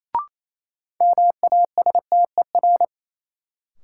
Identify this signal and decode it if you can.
Morse code, heard as a keyed tone, about 28 wpm, 700 Hz. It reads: MASTER
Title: Sentences from Top 1300 Words - New Word Drill Alt - 28wpm